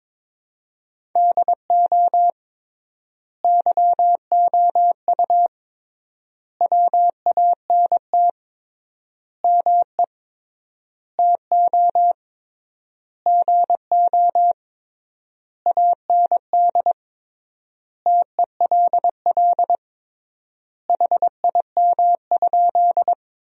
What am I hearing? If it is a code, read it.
Morse code, heard as a keyed tone, about 22 wpm, 700 Hz. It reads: DO YOU WANT ME TO GO AND TELL HIM?
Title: 3x Word-Spacing - Sentences from Top 200 Words - 22wpm